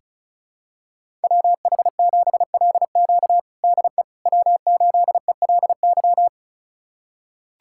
Morse code, heard as a keyed tone, about 36 wpm, 700 Hz. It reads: WH7LQ DE W8ELY
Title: USA Callsign DE Callsign - 5 Characters - 35wpm